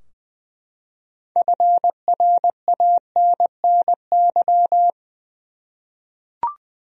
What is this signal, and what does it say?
Morse code, heard as a keyed tone, about 20 wpm, 700 Hz. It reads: FRANNY
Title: USA Common Names - 20wpm